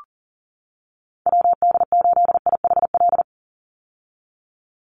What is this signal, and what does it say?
Morse code, heard as a keyed tone, about 40 wpm, 700 Hz. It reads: WD8IHL
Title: USA Call Signs 40wpm